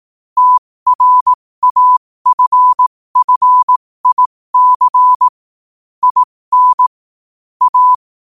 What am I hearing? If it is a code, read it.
Morse code, heard as a keyed tone, about 18 wpm, spaced as for 15 wpm, 1000 Hz. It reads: TRAFFIC IN A
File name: qotdfiles / 15wpm